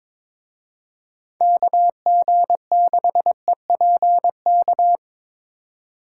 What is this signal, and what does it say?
Morse code, heard as a keyed tone, about 25 wpm, 700 Hz. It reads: KG6EPK
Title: Speed Racing - USA 2x3 Call Signs 22wpm